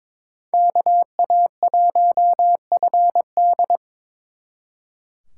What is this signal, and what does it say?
Morse code, heard as a keyed tone, about 26 wpm, 700 Hz. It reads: KA1FD
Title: Speed Racing - USA 2x2 Call Signs 22wpm